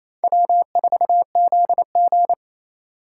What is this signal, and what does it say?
Morse code, heard as a keyed tone, about 28 wpm, 700 Hz. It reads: W4ZG
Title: USA Call Signs - 4 Characters - 3x Character Spacing - 28wpm